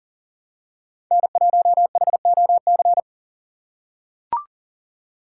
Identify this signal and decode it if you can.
Morse code, heard as a keyed tone, about 40 wpm, 700 Hz. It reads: N1HOC
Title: USA Call Signs - 6x Character Spacing - 40wpm